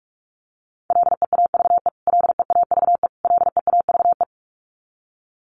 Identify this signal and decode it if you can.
Morse code, heard as a keyed tone, about 44 wpm, 700 Hz. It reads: LEAVE LEAVE LEAVE
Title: Top 500 Words Repeated 45wpm